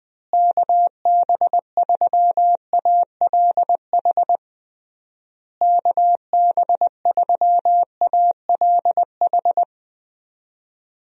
Morse code, heard as a keyed tone, about 20 wpm, 700 Hz. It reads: KB3ALH KB3ALH
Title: USA Call Signs - 6 Characters - Repeated 3x - 20wpm